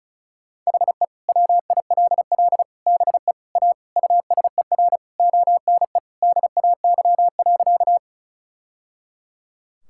Morse code, heard as a keyed tone, about 36 wpm, 700 Hz. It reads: HE WILL BE A USER ONE DAY.
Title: Sentences from Top 1000 Words - Review Part 5 - 35wpm